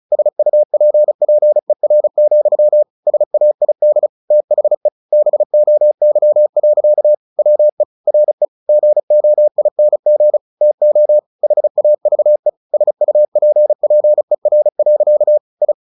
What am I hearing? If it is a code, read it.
Morse code, heard as a keyed tone, about 34 wpm, 600 Hz. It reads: SUPPER, SAID THE BOY. WE RE GOING TO HAVE SUPPER. I
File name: the old man and the sea / the old man and the sea 1 35wpm